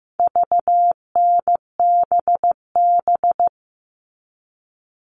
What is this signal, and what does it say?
Morse code, heard as a keyed tone, about 15 wpm, 700 Hz. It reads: VNBB
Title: All Cut Numbers - 4-Digit 15wpm